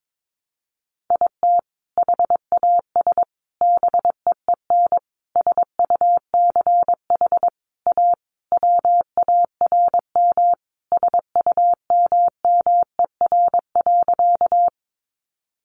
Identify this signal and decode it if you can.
Morse code, heard as a keyed tone, about 22 wpm, 700 Hz. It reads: IT HAS BEEN SUCH A WARM SUMMER.
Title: Sentences from Top 1100 Words 22wpm